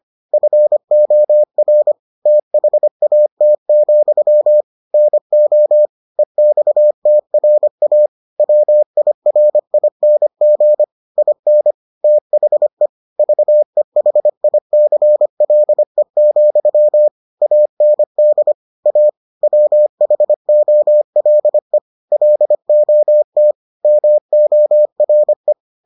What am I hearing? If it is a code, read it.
Morse code, heard as a keyed tone, about 25 wpm, 600 Hz. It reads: FOR THAT, NO EXTRA WIRING IN THE VEHICLE, AND A WHOLE LOT MORE